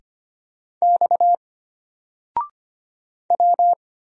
Morse code, heard as a keyed tone, about 25 wpm, 700 Hz. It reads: X W
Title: Single Character (Letter:Number:Punctuation:Prosign) Alt 25wpm